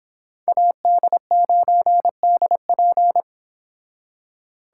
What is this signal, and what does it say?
Morse code, heard as a keyed tone, about 30 wpm, 700 Hz. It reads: AD9DP